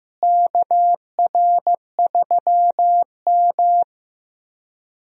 Morse code, heard as a keyed tone, about 15 wpm, 700 Hz. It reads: KR3M
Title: USA Call Signs - 4 Characters - Repeated 3x - 15wpm